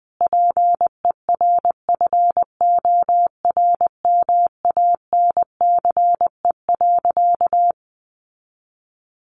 Morse code, heard as a keyed tone, about 20 wpm, 700 Hz. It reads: PERFORMANCE.
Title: Sentences from Top 1000 Words 20wpm